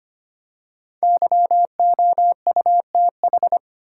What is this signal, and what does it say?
Morse code, heard as a keyed tone, about 25 wpm, 700 Hz. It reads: YOUTH
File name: 5 Letter Words - First 3 Letters Given - 25wpm